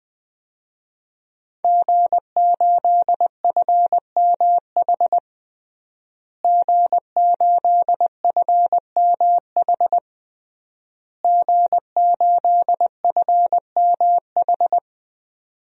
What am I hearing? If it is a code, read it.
Morse code, heard as a keyed tone, about 21 wpm, 700 Hz. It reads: G8FMH G8FMH G8FMH
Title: DXCC-100 Call Signs - 5 Characters - Repeated 3x - 20wpm